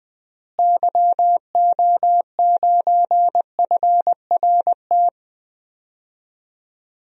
Morse code, heard as a keyed tone, about 21 wpm, 700 Hz. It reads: YO9FRT
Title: DXCC-100 Call Signs - 6 Characters - Repeated 3x - 20wpm